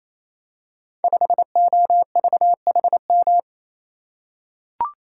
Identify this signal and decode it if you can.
Morse code, heard as a keyed tone, about 27 wpm, 700 Hz. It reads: 5OVHM
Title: Code Groups - 5 Letter:Numbers - 28wpm